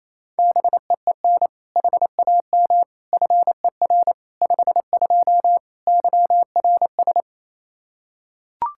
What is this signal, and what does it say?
Morse code, heard as a keyed tone, about 28 wpm, 700 Hz. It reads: BEEN HAM FER 52 YRS
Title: Practice QSO - Exchange 3 - More Info - 28wpm